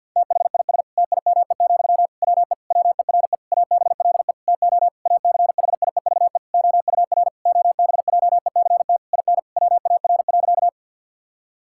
Morse code, heard as a keyed tone, about 50 wpm, 700 Hz. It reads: THIS TIME, WE WERE ABLE TO ACHIEVE OUR OBJECT IN WAR.
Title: Sentences from Top 1000 Words - Review Part 6 - 50wpm